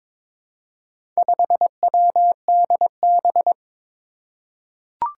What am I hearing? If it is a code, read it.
Morse code, heard as a keyed tone, about 22 wpm, 700 Hz. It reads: 5WDB
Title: Four Challenging Letter-Number Combinations 22wpm